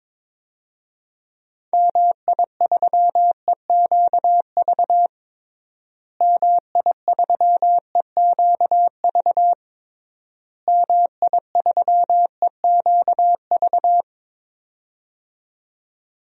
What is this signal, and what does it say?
Morse code, heard as a keyed tone, about 22 wpm, 700 Hz. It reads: MI3EQV MI3EQV MI3EQV
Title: DXCC-100 Call Signs - 6 Characters - Repeated 3x - 22wpm